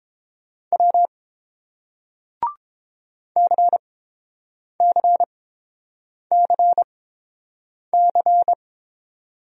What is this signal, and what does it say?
Morse code, heard as a keyed tone, about 31 wpm, spaced as for 7 wpm, 700 Hz. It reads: W CCCC